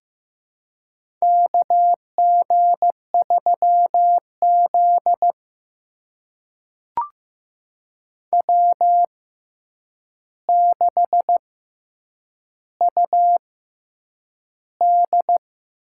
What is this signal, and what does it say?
Morse code, heard as a keyed tone, about 15 wpm, spaced as for 8 wpm, 700 Hz. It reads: KG3Z W6UD
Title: USA Call Signs - 6x Character Spacing - 15wpm